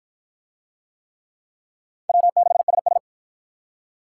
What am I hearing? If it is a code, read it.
Morse code, heard as a keyed tone, about 53 wpm, 700 Hz. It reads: W6SS